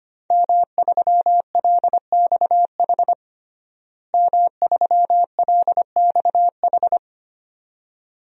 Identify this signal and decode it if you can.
Morse code, heard as a keyed tone, about 25 wpm, 700 Hz. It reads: M3LXH M3LXH
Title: DXCC-100 Call Signs - 5 Characters - Repeated 2x - 25wpm